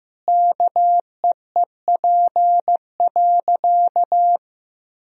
Morse code, heard as a keyed tone, about 15 wpm, 700 Hz. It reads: KEEP.
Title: Sentences from Top 1000 Words - Review Part 2 - 15wpm